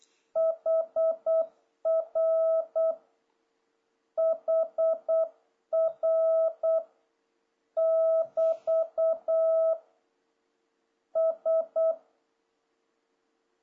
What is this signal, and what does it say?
Morse code, heard as a keyed tone, about 8 wpm, 650 Hz. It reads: HR HR <BT> S
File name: jh8cbh / 2026cw